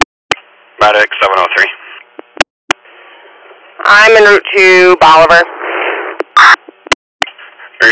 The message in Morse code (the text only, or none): none